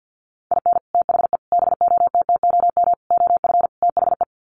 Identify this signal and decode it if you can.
Morse code, heard as a keyed tone, about 50 wpm, 700 Hz. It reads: IN THE BOTTOM OF THE
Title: Sets of 5 Words Encore 50wpm